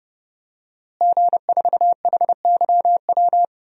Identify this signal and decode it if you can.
Morse code, heard as a keyed tone, about 32 wpm, 700 Hz. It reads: G4HYW
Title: DXCC-100 Call Signs - 5 Characters - Repeated 3x - 30wpm